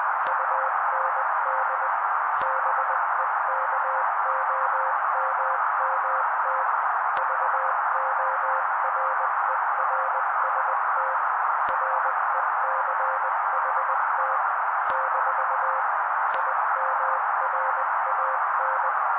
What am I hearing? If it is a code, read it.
Morse code, heard as a keyed tone, about 20 wpm, 530 Hz. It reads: UND BEKOMMT VORERST RECHT - IMRAN
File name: daily / 18wpm noise